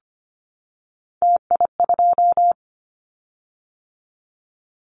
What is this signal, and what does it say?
Morse code, heard as a keyed tone, about 25 wpm, 700 Hz. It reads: TI2
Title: Callsign Prefixes for Top 100 DXCC - 25wpm